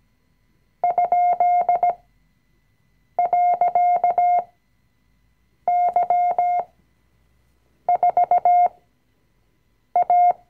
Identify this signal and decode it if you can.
Morse code, heard as a keyed tone, about 17 wpm, spaced as for 6 wpm, 710 Hz. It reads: ?.Y4A
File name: CW-Online-Kurs / Anfänger / Lektion 020